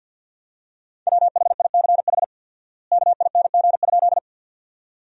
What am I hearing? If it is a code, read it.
Morse code, heard as a keyed tone, about 50 wpm, 700 Hz. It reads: WHICH KING?
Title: 4x Word-Spacing - Sentences from Top 400 Words - 50wpm